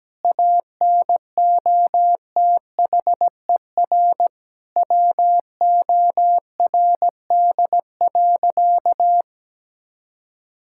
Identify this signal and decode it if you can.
Morse code, heard as a keyed tone, about 17 wpm, 700 Hz. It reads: ANOTHER WORD.
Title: Sentences from Top 1000 Words - Review Part 6 - 17wpm